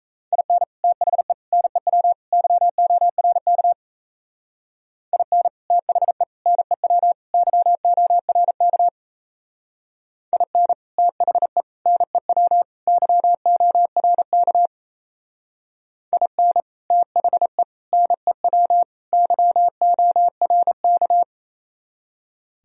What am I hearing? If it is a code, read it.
Morse code, heard as a keyed tone, about 35 wpm, 700 Hz. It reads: IN THE NEW YORK IN THE NEW YORK IN THE NEW YORK IN THE NEW YORK
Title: Speed-Racing - Sets of 4 Words 28wpm